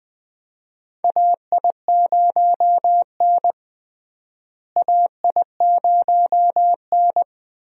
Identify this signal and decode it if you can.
Morse code, heard as a keyed tone, about 20 wpm, 700 Hz. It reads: AI0N AI0N
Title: USA Call Signs - 4 Characters - Repeated 2x - 20wpm